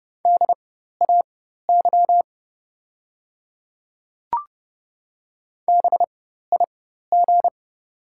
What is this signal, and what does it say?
Morse code, heard as a keyed tone, about 30 wpm, spaced as for 14 wpm, 690 Hz. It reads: DAY BIG
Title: 4x Character Spacing - Top 200 Words - 30wpm